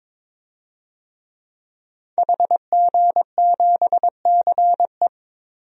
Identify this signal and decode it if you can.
Morse code, heard as a keyed tone, about 22 wpm, 700 Hz. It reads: HG7CE
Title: DXCC-100 Call Signs - 5 Characters - Repeated 3x - 22wpm